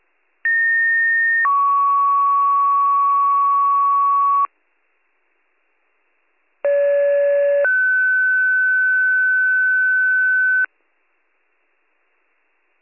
Two-tone page, heard as a one-second tone, then a three-second tone, about 1850 then 1120 Hz, about 570 then 1500 Hz. It is heard from 0.4 to 10.7 s.